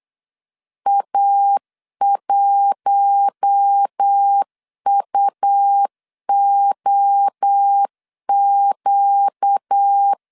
Morse code, heard as a keyed tone, about 8 wpm, 800 Hz. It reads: A1UOQ
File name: 8wpm